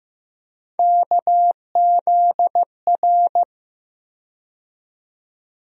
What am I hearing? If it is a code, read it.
Morse code, heard as a keyed tone, about 15 wpm, 700 Hz. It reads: KZR
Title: USA Call Sign Postfixes - 3 Letters 15wpm